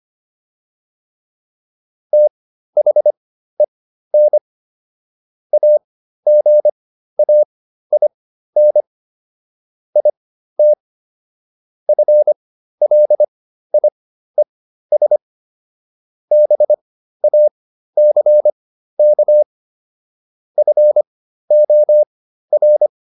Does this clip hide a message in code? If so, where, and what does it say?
Yes: Morse code from 2.1 to 22.9 s, heard as a keyed tone, about 25 wpm, spaced as for 13 wpm, 600 Hz. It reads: THEN AGAIN IT FLIES BACK FOR